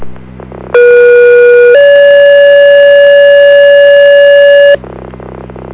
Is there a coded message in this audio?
Yes: two-tone page from 0.7 to 4.8 s, heard as a one-second tone, then a three-second tone, about 480 then 570 Hz, with noise.